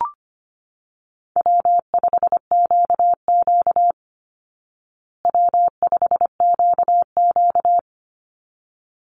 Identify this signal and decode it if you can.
Morse code, heard as a keyed tone, about 25 wpm, 700 Hz. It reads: W5QQ W5QQ
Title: Call Signs Repeated 3x - 25wpm